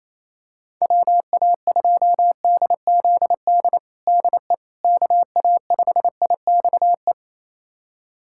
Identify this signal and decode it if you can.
Morse code, heard as a keyed tone, about 27 wpm, 700 Hz. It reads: WA2DZD DE KA5IXE
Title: USA Callsign DE Callsign - 6 Characters - 28wpm